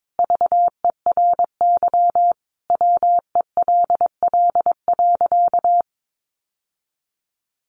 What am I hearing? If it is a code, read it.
Morse code, heard as a keyed tone, about 22 wpm, 700 Hz. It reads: VERY WELL.
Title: Sentences from Top 1400 Words 22wpm